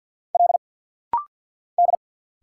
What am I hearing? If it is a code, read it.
Morse code, heard as a keyed tone, about 52 wpm, 700 Hz. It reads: L D
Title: Single Letters - Rapid-Fire 50wpm